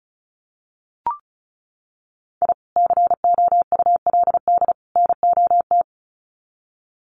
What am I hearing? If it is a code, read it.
Morse code, heard as a keyed tone, about 34 wpm, 700 Hz. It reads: I COULD NOT
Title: Sets of 3 Words 35wpm